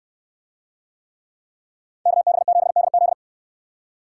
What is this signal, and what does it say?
Morse code, heard as a keyed tone, about 66 wpm, 700 Hz. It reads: KB7DZ